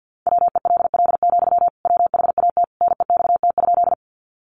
Morse code, heard as a keyed tone, about 49 wpm, 700 Hz. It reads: WELL, WHAT NEXT?
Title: Sentences from Top 300 Words 50wpm